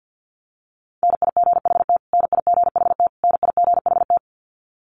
Morse code, heard as a keyed tone, about 50 wpm, 700 Hz. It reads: NIGHT NIGHT NIGHT
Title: Top 500 Words Repeated 50wpm